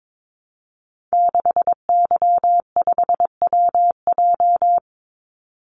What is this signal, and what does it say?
Morse code, heard as a keyed tone, about 22 wpm, 700 Hz. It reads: 6Y5WJ
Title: Call Signs Repeated 3x - 22wpm